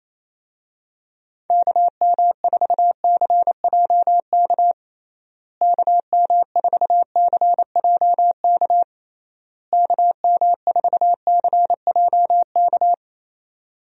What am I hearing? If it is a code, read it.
Morse code, heard as a keyed tone, about 28 wpm, 700 Hz. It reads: KM4CJK KM4CJK KM4CJK
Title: USA Call Signs - 6 Characters - Repeated 3x - 28wpm